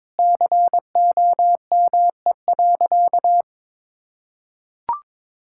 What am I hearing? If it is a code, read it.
Morse code, heard as a keyed tone, about 22 wpm, 700 Hz. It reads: COME.
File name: practice / 22wpm / Sentences with 2-Word Phrases - 22wpm